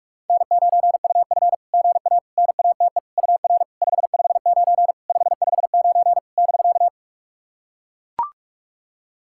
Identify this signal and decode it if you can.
Morse code, heard as a keyed tone, about 44 wpm, 690 Hz. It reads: N9UF GA NATE UR 559 559 <BK>
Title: SOTA QSO Practice 45wpm